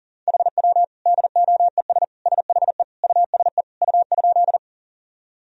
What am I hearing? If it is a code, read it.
Morse code, heard as a keyed tone, about 40 wpm, 700 Hz. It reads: HW DOES SHE USE U?
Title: Sentences from Top 200 Words and Abbreviations 40wpm